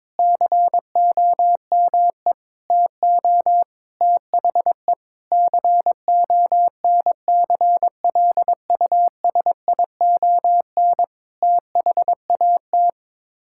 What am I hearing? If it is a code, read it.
Morse code, heard as a keyed tone, about 22 wpm, 700 Hz. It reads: COME TO THE CONCLUSION THAT
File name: practice / 22wpm / Sets of 7 Words - 22wpm